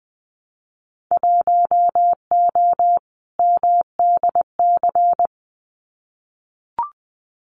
Morse code, heard as a keyed tone, about 20 wpm, 700 Hz. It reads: 1O MDC
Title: Winter Field Day Exchanges 20wpm